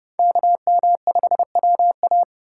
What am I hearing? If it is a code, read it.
Morse code, heard as a keyed tone, about 31 wpm, 700 Hz. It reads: KM5WA
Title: USA Call Signs - 2x Character Spacing - 30wpm